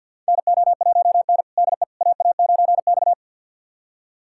Morse code, heard as a keyed tone, about 46 wpm, 700 Hz. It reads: NO1N DE AA9X
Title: USA Callsign DE Callsign - 4 Characters - 50wpm